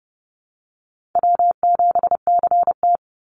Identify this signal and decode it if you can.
Morse code, heard as a keyed tone, about 30 wpm, 700 Hz. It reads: W7CT
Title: Call Signs - 30wpm